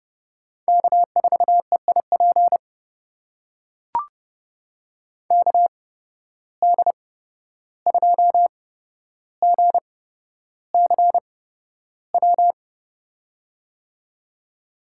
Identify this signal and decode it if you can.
Morse code, heard as a keyed tone, about 29 wpm, spaced as for 12 wpm, 700 Hz. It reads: K4EIP KD2GCW
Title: USA Call Signs - 8x Character Spacing - 30wpm